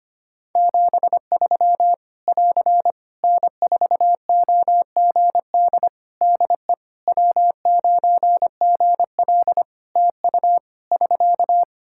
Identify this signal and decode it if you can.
Morse code, heard as a keyed tone, about 25 wpm, 700 Hz. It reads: 73 <AR> N4OGD DE W9GL TU <SK>
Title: K1USN SST QSO Practice 25wpm